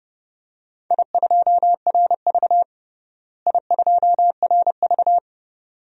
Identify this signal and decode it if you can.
Morse code, heard as a keyed tone, about 30 wpm, 700 Hz. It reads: I2RV I2RV
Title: DXCC-100 Call Signs - 4 Characters - Repeated 2x - 30wpm